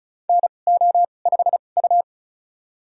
Morse code, heard as a keyed tone, about 34 wpm, spaced as for 25 wpm, 700 Hz. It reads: NO5U
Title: USA Call Signs - 4 Characters - 2x Character Spacing - 35wpm